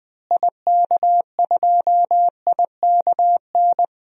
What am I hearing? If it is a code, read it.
Morse code, heard as a keyed tone, about 20 wpm, 700 Hz. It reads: IK2IKN
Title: DXCC-100 Call Signs - 6 Characters - Repeated 3x - 20wpm